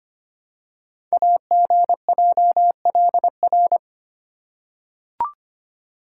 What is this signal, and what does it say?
Morse code, heard as a keyed tone, about 25 wpm, 700 Hz. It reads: AGJLR
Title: Code Groups - 5 Letters - 25wpm